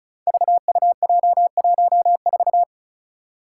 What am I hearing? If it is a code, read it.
Morse code, heard as a keyed tone, about 36 wpm, 700 Hz. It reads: VUJ14
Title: Five Challenging Letter-Number Combinations 35wpm